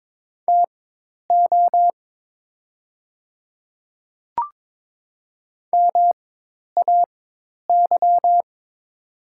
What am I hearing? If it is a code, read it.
Morse code, heard as a keyed tone, about 22 wpm, spaced as for 10 wpm, 700 Hz. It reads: TO MAY
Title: 4x Character Spacing - Top 100 Words - 22wpm